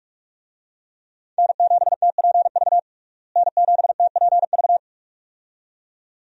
Morse code, heard as a keyed tone, about 41 wpm, 700 Hz. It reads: N7TPV N7TPV
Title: USA Call Signs - 5 Characters - Repeated 2x - 45wpm